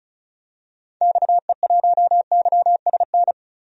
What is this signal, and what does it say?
Morse code, heard as a keyed tone, about 36 wpm, 700 Hz. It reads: XE1YSN
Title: DXCC-100 Call Signs - 6 Characters - 35wpm